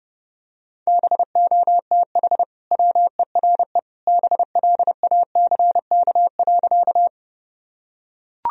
Morse code, heard as a keyed tone, about 32 wpm, 700 Hz. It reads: BOTH WERE BLACK.
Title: Sentences from Top 1000 Words - Review Part 2 - 30wpm